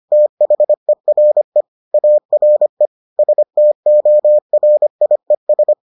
Morse code, heard as a keyed tone, about 25 wpm, 600 Hz. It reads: THERE ARE STORIES